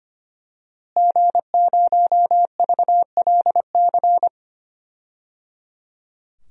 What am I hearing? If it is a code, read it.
Morse code, heard as a keyed tone, about 25 wpm, 700 Hz. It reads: G0VLC
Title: DXCC-100 Call Signs - 5 Characters - Repeated 3x - 25wpm